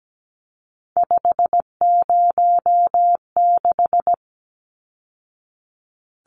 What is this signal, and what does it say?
Morse code, heard as a keyed tone, about 17 wpm, 700 Hz. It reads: 506
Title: Numbers - 3-Digit 17wpm